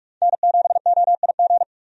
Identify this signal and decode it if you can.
Morse code, heard as a keyed tone, about 42 wpm, 700 Hz. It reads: N7OIG
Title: USA Call Signs - 5 Characters - Repeated 3x - 45wpm